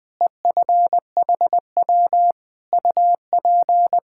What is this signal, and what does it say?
Morse code, heard as a keyed tone, about 20 wpm, 700 Hz. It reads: EFHW UP
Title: K1USN SST QSO Practice 20wpm